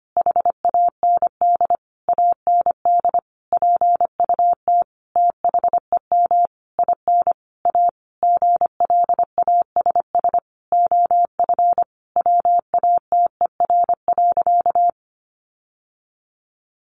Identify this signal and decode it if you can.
Morse code, heard as a keyed tone, about 25 wpm, 700 Hz. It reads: HAND AND PUT THEM IN A GLASS OF WATER.
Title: Sentences from Top 1300 Words 25wpm